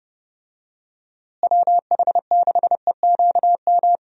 Morse code, heard as a keyed tone, about 31 wpm, 700 Hz. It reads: WH6EQM
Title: USA Call Signs - 6 Characters - Repeated 2x - 30wpm